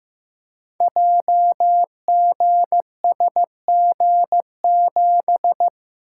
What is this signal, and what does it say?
Morse code, heard as a keyed tone, about 15 wpm, 700 Hz. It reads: JGSG7
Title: Code Groups - 5 Letter:Numbers - 15wpm